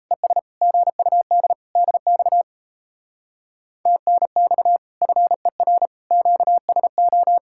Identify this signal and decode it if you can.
Morse code, heard as a keyed tone, about 38 wpm, 700 Hz. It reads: ES GUD DX TNX FER QSO